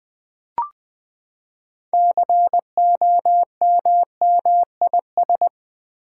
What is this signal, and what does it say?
Morse code, heard as a keyed tone, about 20 wpm, 700 Hz. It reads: COMMIS
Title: Build A Word - 6 Letters At A Time - 12-Letter Words - 20wpm